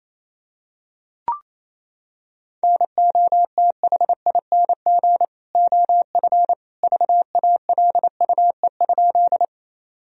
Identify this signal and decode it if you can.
Morse code, heard as a keyed tone, about 28 wpm, 700 Hz. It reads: NOTHING OF VALUE?
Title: Sentences from Top 1000 Words - Review Part 3 - 28wpm